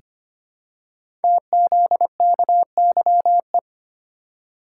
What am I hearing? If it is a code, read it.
Morse code, heard as a keyed tone, about 25 wpm, 700 Hz. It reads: TZKYE
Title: Code Groups - 5 Letters - 25wpm